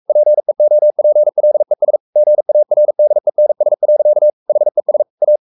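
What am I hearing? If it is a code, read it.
Morse code, heard as a keyed tone, about 44 wpm, 600 Hz. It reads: PEOPLES GARDENS. HES A
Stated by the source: Morse code at 43 wpm sending Ken Follett's novel Never in English, from CW med Gnister.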